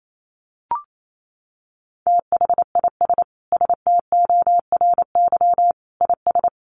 Morse code, heard as a keyed tone, about 28 wpm, 700 Hz. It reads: THIS STORY IS
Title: Sentences from Top 1300 Words 28wpm